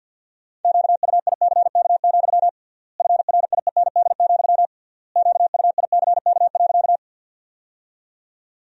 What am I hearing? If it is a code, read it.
Morse code, heard as a keyed tone, about 52 wpm, 700 Hz. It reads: QUICK, FRIEND, QUICK.
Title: 3x Word-Spacing - Sentences from Top 500 Words - 50wpm